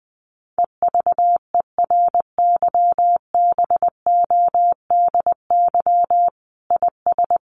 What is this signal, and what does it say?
Morse code, heard as a keyed tone, about 20 wpm, 700 Hz. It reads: EVERYBODY IS
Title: Sentences from Top 1400 Words 20wpm